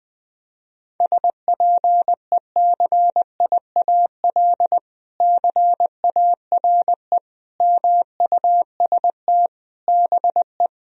Morse code, heard as a keyed tone, about 19 wpm, 700 Hz. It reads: SPECIAL CARE MUST BE
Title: Sentences from Top 1000 Words - Review Part 4 - 20wpm